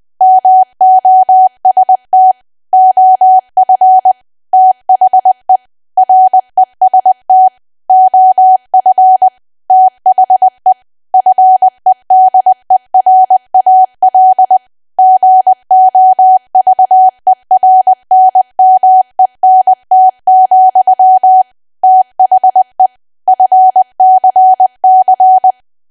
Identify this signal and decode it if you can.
Morse code, heard as a keyed tone, about 20 wpm, 750 Hz. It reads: MOST OF THE REST OF THE FEDERAL GOVERNMENT, THE FCC
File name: cwa / W1AW-Level3 / 20WPM